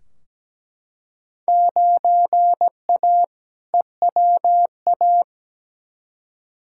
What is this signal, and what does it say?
Morse code, heard as a keyed tone, about 17 wpm, 700 Hz. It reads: 9A EWA
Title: Field Day Exchanges Encore - 17wpm